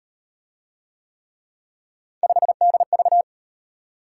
Morse code, heard as a keyed tone, about 38 wpm, 700 Hz. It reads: 5DV